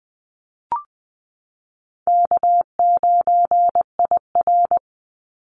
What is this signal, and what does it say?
Morse code, heard as a keyed tone, about 20 wpm, 700 Hz. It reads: K9IR
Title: Easy Call Signs Repeated 2x - 20wpm